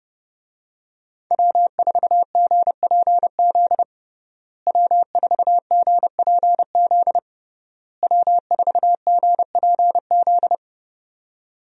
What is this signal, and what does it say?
Morse code, heard as a keyed tone, about 30 wpm, 700 Hz. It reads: W4GPZ W4GPZ W4GPZ
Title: USA Call Signs - 5 Characters - Repeated 3x - 30wpm